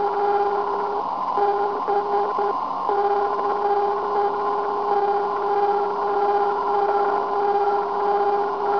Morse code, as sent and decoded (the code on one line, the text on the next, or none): none